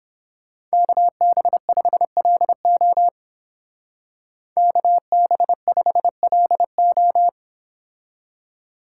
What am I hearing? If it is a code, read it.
Morse code, heard as a keyed tone, about 30 wpm, 700 Hz. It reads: KB5LO KB5LO